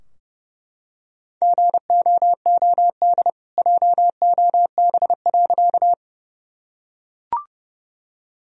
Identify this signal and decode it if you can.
Morse code, heard as a keyed tone, about 30 wpm, 700 Hz. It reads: GOOD JOB.
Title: Sentences from Top 1000 Words - Review Part 3 - 30wpm